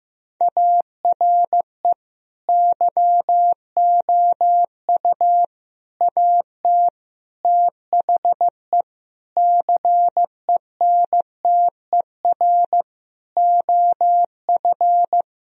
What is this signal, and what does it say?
Morse code, heard as a keyed tone, about 15 wpm, 700 Hz. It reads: ARE YOU AT THE CENTER OF
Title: Sentences from Top 1000 Words - Review Part 2 - 15wpm